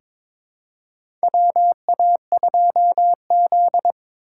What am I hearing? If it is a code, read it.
Morse code, heard as a keyed tone, about 22 wpm, 700 Hz. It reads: WA2Z
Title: USA Call Signs - 4 Characters - 4x Character Spacing - 22wpm